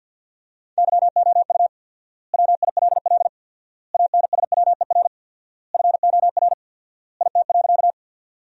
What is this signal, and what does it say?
Morse code, heard as a keyed tone, about 52 wpm, 700 Hz. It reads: YOU WILL ANSWER FOR IT.
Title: 4x Word-Spacing - Sentences from Top 300 Words - 50wpm